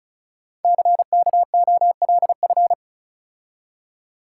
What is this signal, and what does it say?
Morse code, heard as a keyed tone, about 35 wpm, 700 Hz. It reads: CKOLF
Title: Code Groups - 5 Letters - 35wpm